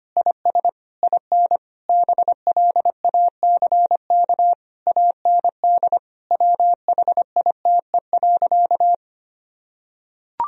Morse code, heard as a keyed tone, about 25 wpm, 700 Hz. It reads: IS IN BLACK AND WHITE.
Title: Sentences from Top 1000 Words - Review Part 2 - 25wpm